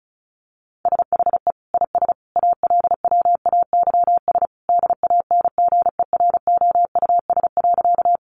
Morse code, heard as a keyed tone, about 34 wpm, 700 Hz. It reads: SHE IS ALWAYS DANGEROUS.
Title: Sentences from Top 1200 Words 35wpm